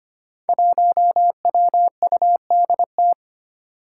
Morse code, heard as a keyed tone, about 25 wpm, 700 Hz. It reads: 1WUDT
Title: Five Challenging Letter-Number Combinations 25wpm